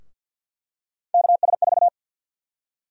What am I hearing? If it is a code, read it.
Morse code, heard as a keyed tone, about 53 wpm, 700 Hz. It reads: KS4
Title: USA Call Sign Prefixes - 50wpm